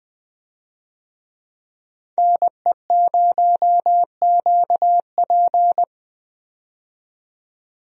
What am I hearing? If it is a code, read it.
Morse code, heard as a keyed tone, about 21 wpm, 700 Hz. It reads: NE0QP
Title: USA Call Signs - 5 Characters - 20wpm